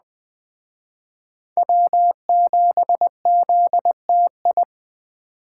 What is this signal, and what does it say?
Morse code, heard as a keyed tone, about 21 wpm, 700 Hz. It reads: W7ZTI